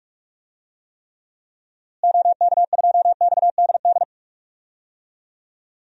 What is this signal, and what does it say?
Morse code, heard as a keyed tone, about 45 wpm, 700 Hz. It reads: OK2XDD
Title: DXCC-100 Call Signs - 6 Characters - Repeated 2x - 45wpm